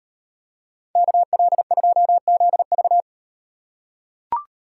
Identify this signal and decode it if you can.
Morse code, heard as a keyed tone, about 38 wpm, 700 Hz. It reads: KL2ZV